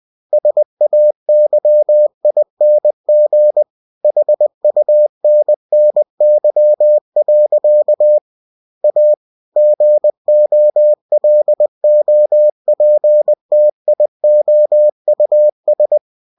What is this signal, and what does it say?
Morse code, heard as a keyed tone, about 20 wpm, 600 Hz. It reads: SAYING HUNNY. A GOLOPTIOUS
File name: cw-books / winnie-the-pooh / winnie-20wpm-page-0039